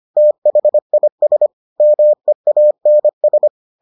Morse code, heard as a keyed tone, about 25 wpm, 600 Hz. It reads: THIS MEANS